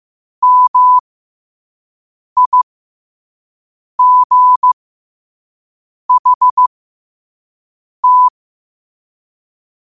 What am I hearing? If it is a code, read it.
Morse code, heard as a keyed tone, about 15 wpm, spaced as for 5 wpm, 1000 Hz. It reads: MIGHT
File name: qotdfiles / 5wpm